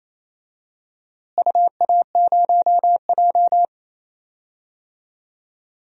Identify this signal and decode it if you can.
Morse code, heard as a keyed tone, about 28 wpm, 700 Hz. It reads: UA0J
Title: DXCC-100 Call Signs - 4 Characters - Repeated 3x - 28wpm